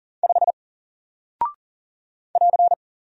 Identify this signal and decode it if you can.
Morse code, heard as a keyed tone, about 39 wpm, 700 Hz. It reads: 5 <AR>
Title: Single Character (Letter:Number:Punctuation:Prosign) Alt 40wpm